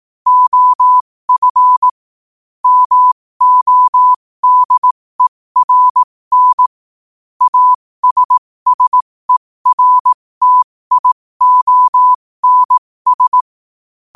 Morse code, heard as a keyed tone, about 18 wpm, spaced as for 15 wpm, 1000 Hz. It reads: OF MODERN ASSERTIONS
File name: qotdfiles / 15wpm